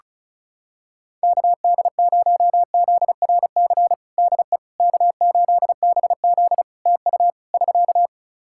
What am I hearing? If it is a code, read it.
Morse code, heard as a keyed tone, about 34 wpm, 700 Hz. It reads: KD0ZRC DE K8BZ TU <SK>
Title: Practice QSO - Exchange 4 - Ending - 35wpm